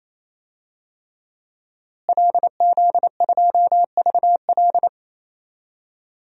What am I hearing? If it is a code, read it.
Morse code, heard as a keyed tone, about 28 wpm, 700 Hz. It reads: LZ2VL
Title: DXCC-100 Call Signs - 5 Characters - Repeated 3x - 28wpm